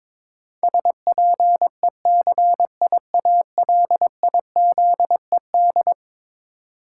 Morse code, heard as a keyed tone, about 22 wpm, 700 Hz. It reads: SPECIALIZED
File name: practice / 22wpm / Eleven Letter Words - 22wpm